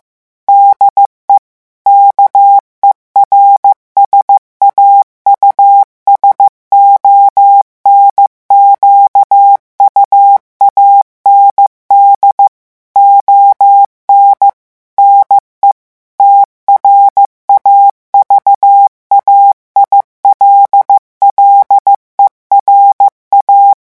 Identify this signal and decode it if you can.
Morse code, heard as a keyed tone, about 15 wpm, 800 Hz. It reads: DE KERSAUSON QUAND ON NE TRAVAILLERA
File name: morse mp3 / revision9 citations humoristiques vitesse 14 mots minute